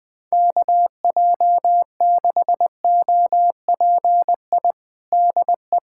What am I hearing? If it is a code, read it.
Morse code, heard as a keyed tone, about 20 wpm, 700 Hz. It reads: KJ6OPI DE
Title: USA Callsign DE Callsign - 6 Characters - 20wpm